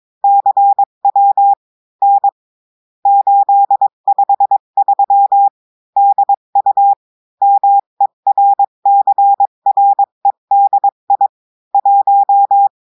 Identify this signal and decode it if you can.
Morse code, heard as a keyed tone, about 22 wpm, 800 Hz. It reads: CW N 853 DU MERCREDI 1
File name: cours-du-mercredi / audios / cours-853-du-2-mai-2024-etxte chiffre et texte clair-a-20-wpm